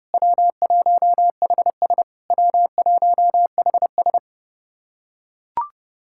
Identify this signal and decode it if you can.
Morse code, heard as a keyed tone, about 32 wpm, 700 Hz. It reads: W1HS W1HS
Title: SOTA QSO Practice 30wpm